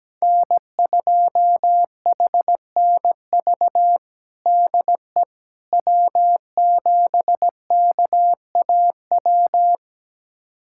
Morse code, heard as a keyed tone, about 17 wpm, 700 Hz. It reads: N2HNV DE W7KAW
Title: USA Callsign DE Callsign - 5 Characters - 17wpm